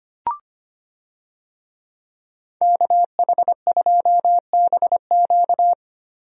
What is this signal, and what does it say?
Morse code, heard as a keyed tone, about 25 wpm, 700 Hz. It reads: KH2BQ
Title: USA Call Signs - 5 Characters - Repeated 2x - 25wpm